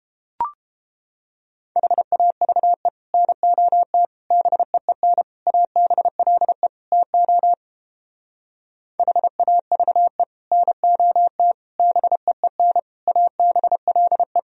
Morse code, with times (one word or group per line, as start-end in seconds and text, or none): 1.8-2.9 HAVE
3.1-4.1 NOT
4.3-5.2 BEEN
5.5-6.7 ABLE
6.9-7.5 TO
9.0-10.2 HAVE
10.5-11.5 NOT
11.8-12.8 BEEN
13.1-14.4 ABLE